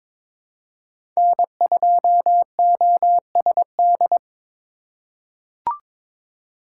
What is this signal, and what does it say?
Morse code, heard as a keyed tone, about 22 wpm, 700 Hz. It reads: N2OSD
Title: USA Call Signs - 5 Characters - 4x Character Spacing - 22wpm